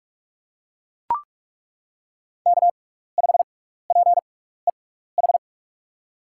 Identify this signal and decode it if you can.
Morse code, heard as a keyed tone, about 43 wpm, spaced as for 15 wpm, 700 Hz. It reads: K5PEH
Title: USA Call Signs - 5 Characters - 6x Character Spacing - 45wpm